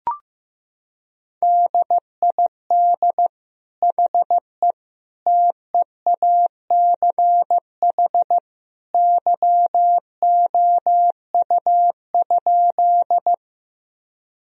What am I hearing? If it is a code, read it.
Morse code, heard as a keyed tone, about 15 wpm, 700 Hz. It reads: DID HE TEACH YOU?
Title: Sentences from Top 1000 Words - Review Part 2 - 15wpm